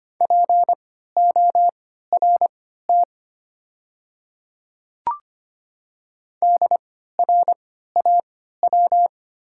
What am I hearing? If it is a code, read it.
Morse code, heard as a keyed tone, about 25 wpm, spaced as for 14 wpm, 690 Hz. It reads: PORT DRAW
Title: 3x Character Spacing - Top 300 Words - 25wpm